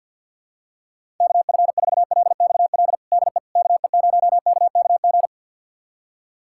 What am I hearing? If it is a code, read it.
Morse code, heard as a keyed tone, about 46 wpm, 700 Hz. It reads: KF4LXL DE KE0KKG
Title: USA Callsign DE Callsign - 6 Characters - 50wpm